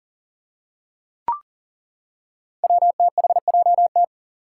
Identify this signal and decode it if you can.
Morse code, heard as a keyed tone, about 40 wpm, 700 Hz. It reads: WTHJT
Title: Five Challenging Letter-Number Combinations 40wpm